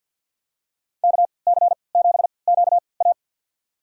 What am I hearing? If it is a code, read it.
Morse code, heard as a keyed tone, about 46 wpm, spaced as for 27 wpm, 700 Hz. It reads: KC7QA
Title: USA Call Signs - 5 Characters - 3x Character Spacing - 50wpm